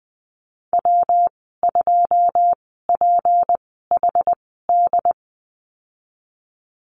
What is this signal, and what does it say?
Morse code, heard as a keyed tone, about 20 wpm, spaced as for 15 wpm, 700 Hz. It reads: W2PHD
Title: 154 Lesson 25g - Letter p - Call Signs 20wpm